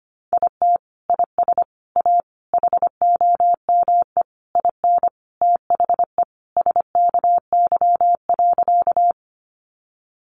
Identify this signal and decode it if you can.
Morse code, heard as a keyed tone, about 25 wpm, 700 Hz. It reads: IT IS A HOME IN THE SKY.
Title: Sentences from Top 1200 Words 25wpm